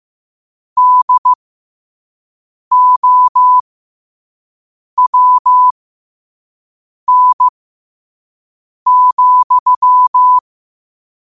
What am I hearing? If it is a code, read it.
Morse code, heard as a keyed tone, about 15 wpm, spaced as for 5 wpm, 1000 Hz. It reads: DOWN,
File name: qotdfiles / 5wpm